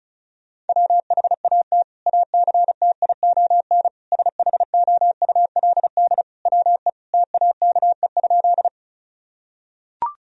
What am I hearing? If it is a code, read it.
Morse code, heard as a keyed tone, about 36 wpm, 700 Hz. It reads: WHAT ACTION SHOULD WE TAKE?
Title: Sentences from Top 1000 Words - Review Part 4 - 35wpm